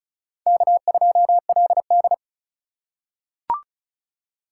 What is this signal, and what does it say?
Morse code, heard as a keyed tone, about 36 wpm, 700 Hz. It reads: K2LD
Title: USA Call Signs - 4 Characters - 4x Character Spacing - 35wpm